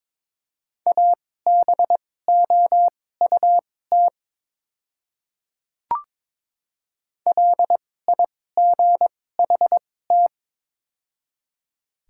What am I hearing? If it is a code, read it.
Morse code, heard as a keyed tone, about 22 wpm, spaced as for 16 wpm, 700 Hz. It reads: ABOUT LIGHT
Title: 2x Character Spacing - Top 300 Words - 22wpm